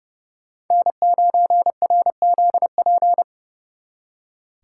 Morse code, heard as a keyed tone, about 37 wpm, 700 Hz. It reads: N9RZP
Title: Speed Racing - USA 1x3 Call Signs 30wpm